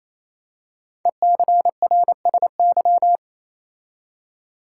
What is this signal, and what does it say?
Morse code, heard as a keyed tone, about 28 wpm, 700 Hz. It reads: ECRSY
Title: Code Groups - 5 Letters - 28wpm